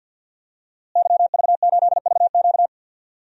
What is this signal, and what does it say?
Morse code, heard as a keyed tone, about 48 wpm, 700 Hz. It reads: YV8VQ
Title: DXCC-100 Call Signs - 5 Characters - Repeated 2x - 50wpm